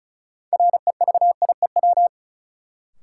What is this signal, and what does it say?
Morse code, heard as a keyed tone, about 36 wpm, 700 Hz. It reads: REVIEW
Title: Sentences from Top 900 Words - New Word Drill - 35wpm